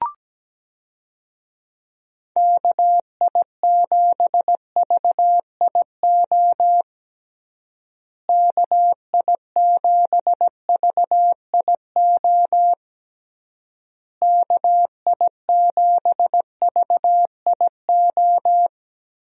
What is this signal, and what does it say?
Morse code, heard as a keyed tone, about 17 wpm, 700 Hz. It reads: KI7VIO KI7VIO KI7VIO
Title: USA Call Signs - 6 Characters - Repeated 3x - 17wpm